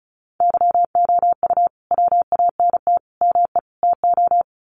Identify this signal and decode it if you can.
Morse code, heard as a keyed tone, about 34 wpm, 700 Hz. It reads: YOU WANT ME TO
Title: Sets of 4 Words 35wpm